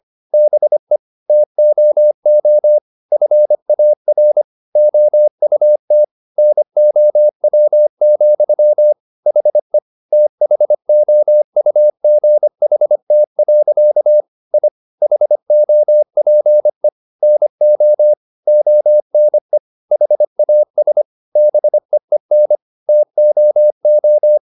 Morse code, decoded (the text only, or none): BE TOO FAR OUT NOW, HE THOUGHT. I HOPE NO ONE HAS BEEN TOO